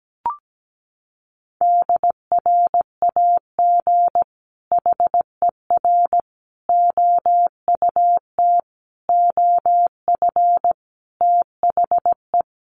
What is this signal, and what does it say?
Morse code, heard as a keyed tone, about 17 wpm, 700 Hz. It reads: DRAG HER OUT OF THE
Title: Sentences from Top 1500 Words 17wpm